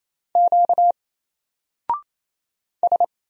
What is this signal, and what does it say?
Morse code, heard as a keyed tone, about 27 wpm, 700 Hz. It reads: Q S
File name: Single-Letter-Number-28wpm